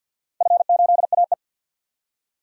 Morse code, heard as a keyed tone, about 52 wpm, 700 Hz. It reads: F8AE
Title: DXCC-100 Call Signs - 4 Characters - 50wpm